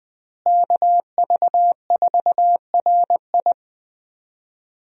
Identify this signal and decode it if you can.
Morse code, heard as a keyed tone, about 20 wpm, 700 Hz. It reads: KV4RI
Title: USA Call Signs - 5 Characters - Repeated 3x - 20wpm